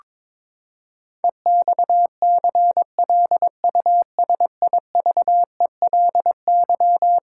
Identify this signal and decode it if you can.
Morse code, heard as a keyed tone, about 22 wpm, 700 Hz. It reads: EXCLUSIVELY
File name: practice / 22wpm / Eleven Letter Words - 22wpm